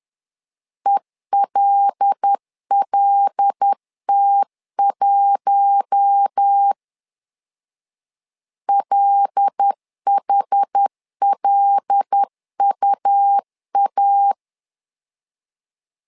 Morse code, heard as a keyed tone, about 10 wpm, 800 Hz. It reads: ELLT1 LHLUA
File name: LEZIONI / 5 LBVC / 5 L 10wpm 2259